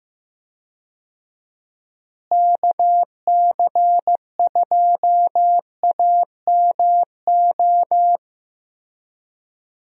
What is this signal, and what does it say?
Morse code, heard as a keyed tone, about 15 wpm, 700 Hz. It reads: KC2AMO
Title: USA Call Signs - 6 Characters - Repeated 2x - 15wpm